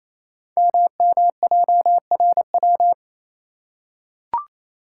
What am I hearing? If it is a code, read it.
Morse code, heard as a keyed tone, about 27 wpm, 700 Hz. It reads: MMJRW
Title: Code Groups - 5 Letters - 28wpm